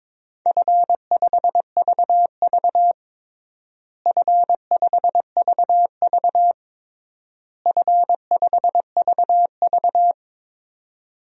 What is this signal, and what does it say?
Morse code, heard as a keyed tone, about 22 wpm, 700 Hz. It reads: F5VV F5VV F5VV
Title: DXCC-100 Call Signs - 4 Characters - Repeated 3x - 22wpm